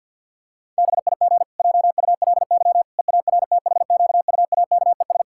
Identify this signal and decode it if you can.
Morse code, heard as a keyed tone, about 50 wpm, 700 Hz. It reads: BIG JULY EARTHQUAKES
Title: Pangram Challenge v2 - 50wpm